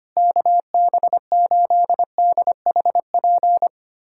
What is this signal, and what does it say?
Morse code, heard as a keyed tone, about 25 wpm, 700 Hz. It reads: KB8DHP
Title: USA Callsign DE Callsign 25wpm